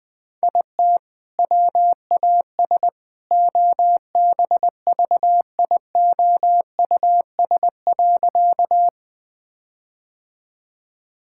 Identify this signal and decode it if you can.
Morse code, heard as a keyed tone, about 20 wpm, 700 Hz. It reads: IT WAS OBVIOUS.
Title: Sentences from Top 1600 Words 20wpm